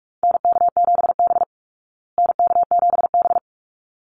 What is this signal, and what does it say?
Morse code, heard as a keyed tone, about 44 wpm, 700 Hz. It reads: NK7B NK7B
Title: Call Signs Repeated 3x - 45wpm